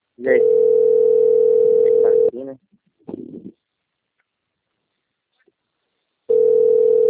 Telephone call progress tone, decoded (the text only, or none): ringback tone